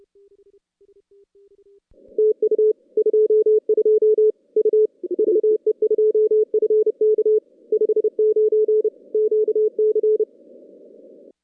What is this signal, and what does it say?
Morse code, heard as a keyed tone, about 31 wpm, 440 Hz. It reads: TU 22 U VE2FK 59 QC
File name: contests / CW-SS-k5tr- -2100